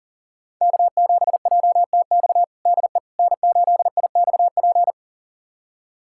Morse code, heard as a keyed tone, about 40 wpm, 700 Hz. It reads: K7JTX DE N8IXP
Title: USA Callsign DE Callsign - 5 Characters - 40wpm